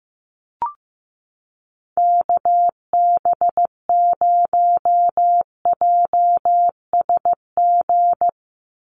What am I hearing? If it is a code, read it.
Morse code, heard as a keyed tone, about 15 wpm, 700 Hz. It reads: KB0JSG
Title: USA Call Signs 15wpm